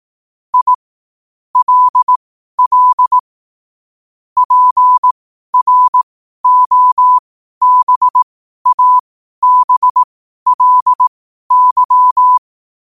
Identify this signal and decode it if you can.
Morse code, heard as a keyed tone, about 18 wpm, spaced as for 12 wpm, 1000 Hz. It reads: I LL PROBABLY
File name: qotdfiles / 12wpm